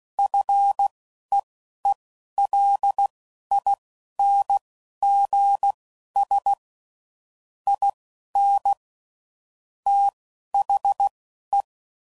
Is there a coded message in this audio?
Yes: Morse code from 0.2 to 11.6 s, heard as a keyed tone, about 16 wpm, spaced as for 11 wpm, 800 Hz. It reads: FEELINGS IN THE